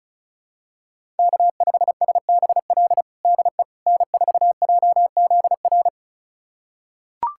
Morse code, heard as a keyed tone, about 36 wpm, 700 Hz. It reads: K5SBL DE N4JZR
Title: USA Callsign DE Callsign - 5 Characters - 35wpm